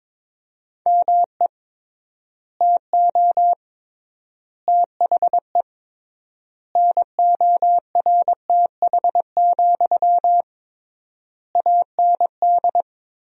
Morse code, heard as a keyed tone, about 22 wpm, 700 Hz. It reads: ME TO THE NORTH, AND
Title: 3x Word-Spacing - Sentences from Top 400 Words - 22wpm